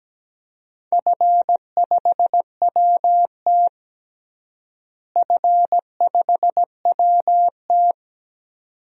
Morse code, heard as a keyed tone, about 17 wpm, 700 Hz. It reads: F5WT F5WT
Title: DXCC-100 Call Signs - 4 Characters - Repeated 2x - 17wpm